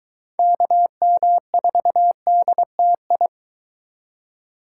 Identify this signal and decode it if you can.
Morse code, heard as a keyed tone, about 23 wpm, 700 Hz. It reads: KM4DTI